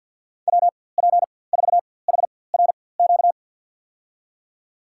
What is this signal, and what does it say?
Morse code, heard as a keyed tone, about 51 wpm, spaced as for 23 wpm, 700 Hz. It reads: WP4HRQ
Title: USA Call Signs - 6 Characters - 4x Character Spacing - 50wpm